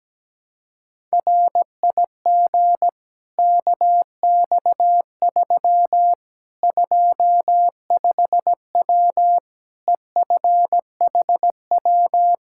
Morse code, heard as a keyed tone, about 17 wpm, 700 Hz. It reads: RIG KX3 25W EFHW
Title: Practice QSO - Exchange 2 - Info - 17wpm